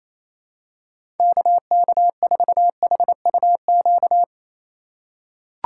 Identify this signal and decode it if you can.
Morse code, heard as a keyed tone, about 28 wpm, 700 Hz. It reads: KK4HUQ
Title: USA Call Signs - 6 Characters - 3x Character Spacing - 28wpm